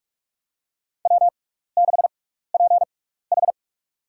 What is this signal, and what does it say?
Morse code, heard as a keyed tone, about 46 wpm, spaced as for 16 wpm, 700 Hz. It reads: W6PH
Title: USA Call Signs - 4 Characters - 6x Character Spacing - 45wpm